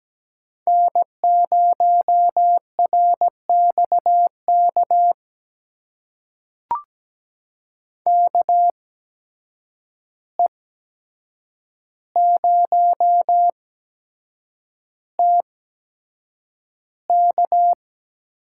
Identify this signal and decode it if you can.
Morse code, heard as a keyed tone, about 17 wpm, spaced as for 7 wpm, 700 Hz. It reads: N0RXK KE0TK
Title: USA Call Signs - 8x Character Spacing - 17wpm